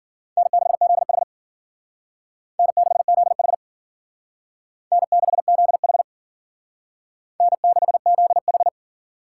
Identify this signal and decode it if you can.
Morse code, heard as a keyed tone, about 50 wpm, 700 Hz. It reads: N6ZH N6ZH N6ZH N6ZH